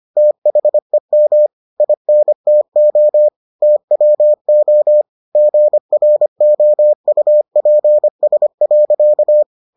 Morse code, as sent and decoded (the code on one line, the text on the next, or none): - .... -- / .. -. - --- / - .-- --- / --. .-. --- ..- .--. ... .-.-.-
THEM INTO TWO GROUPS.